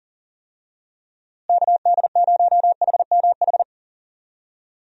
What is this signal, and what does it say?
Morse code, heard as a keyed tone, about 40 wpm, 700 Hz. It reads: KD0HMH
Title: USA Call Signs - 6 Characters - Repeated 2x - 40wpm